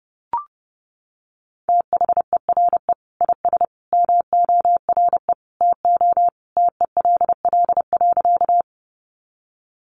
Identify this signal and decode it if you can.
Morse code, heard as a keyed tone, about 29 wpm, 700 Hz. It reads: THERE IS MORE TO TELL.
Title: Sentences from Top 200 Words 30wpm